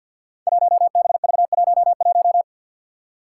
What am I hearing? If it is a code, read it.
Morse code, heard as a keyed tone, about 52 wpm, 700 Hz. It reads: 1BV11
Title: Code Groups - 5 Letter:Numbers - 50wpm